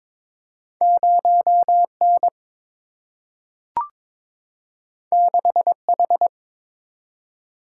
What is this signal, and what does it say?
Morse code, heard as a keyed tone, about 22 wpm, 700 Hz. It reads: 0 N 6 H
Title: Single-Letter Single-Number 22wpm